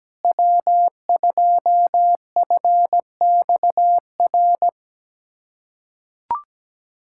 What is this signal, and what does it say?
Morse code, heard as a keyed tone, about 17 wpm, 700 Hz. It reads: W2FXR
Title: USA Callsign DE Callsign - 5 Characters - 17wpm